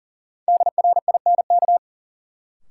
Morse code, heard as a keyed tone, about 40 wpm, 700 Hz. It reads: DRINK
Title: Sentences from Top 1500 Words - New Word Drill - 40wpm